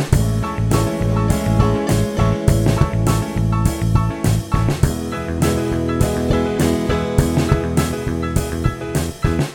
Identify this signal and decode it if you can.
There is no code.